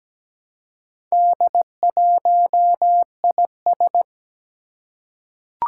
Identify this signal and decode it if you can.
Morse code, heard as a keyed tone, about 17 wpm, 700 Hz. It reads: D1IS
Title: Four Challenging Letter-Number Combinations 17wpm